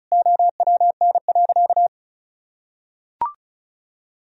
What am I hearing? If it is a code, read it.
Morse code, heard as a keyed tone, about 34 wpm, 700 Hz. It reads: OWN.
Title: Sentences from Top 1000 Words - Review Part 2 - 35wpm